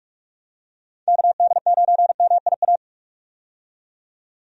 Morse code, heard as a keyed tone, about 46 wpm, 700 Hz. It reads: KD9MIA
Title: USA Call Signs - 6 Characters - 45wpm